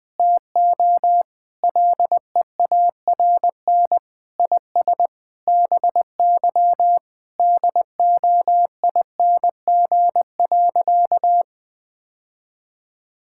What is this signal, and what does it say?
Morse code, heard as a keyed tone, about 21 wpm, 700 Hz. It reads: TO LEARN IS BY DOING.
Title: Sentences with 6-Word Phrases - 20wpm